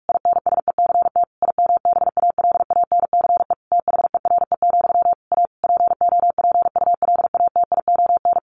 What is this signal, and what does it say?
Morse code, heard as a keyed tone, about 44 wpm, 700 Hz. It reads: INSECT IMBALANCE THERE, A POPULATION
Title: A Sound Of Thunder 45wpm